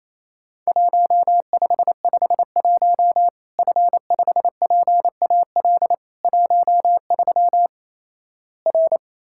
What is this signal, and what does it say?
Morse code, heard as a keyed tone, about 28 wpm, 700 Hz. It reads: 1551 F5PAL 13 R
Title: Worked All Europe DX Contest - QTC Exchange - 28wpm